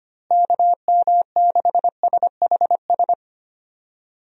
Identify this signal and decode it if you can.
Morse code, heard as a keyed tone, about 25 wpm, 700 Hz. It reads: KM6SHS
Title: USA Call Signs - 6 Characters - Repeated 2x - 25wpm